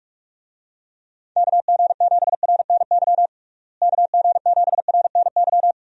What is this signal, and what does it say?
Morse code, heard as a keyed tone, about 46 wpm, 700 Hz. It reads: KG7RNY KG7RNY
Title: USA Call Signs - 6 Characters - Repeated 3x - 45wpm